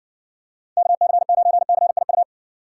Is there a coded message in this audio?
Yes: Morse code from 0.8 to 2.2 s, heard as a keyed tone, about 60 wpm, 700 Hz. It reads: KC9CIU